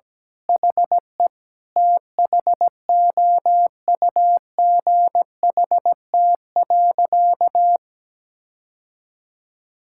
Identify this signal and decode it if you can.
Morse code, heard as a keyed tone, about 17 wpm, 700 Hz. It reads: HE THOUGHT.
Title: Sentences from Top 1600 Words 17wpm